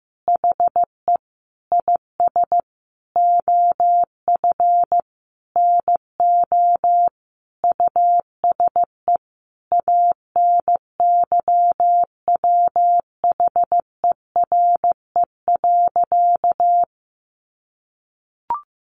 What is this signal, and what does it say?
Morse code, heard as a keyed tone, about 15 wpm, 700 Hz. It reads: HE IS OF NO USE ANYWHERE.
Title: Sentences from Top 1400 Words 15wpm